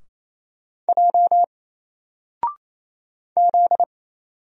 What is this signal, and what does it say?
Morse code, heard as a keyed tone, about 27 wpm, 700 Hz. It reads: J Z
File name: Single-Letter-Number-28wpm